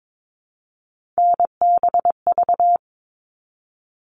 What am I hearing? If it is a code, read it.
Morse code, heard as a keyed tone, about 22 wpm, 700 Hz. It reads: NBV
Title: All Cut Numbers - 3-Digit 22wpm